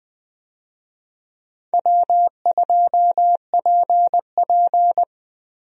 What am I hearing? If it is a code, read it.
Morse code, heard as a keyed tone, about 21 wpm, 700 Hz. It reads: W2PP
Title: USA Call Signs - 4 Characters - Repeated 3x - 20wpm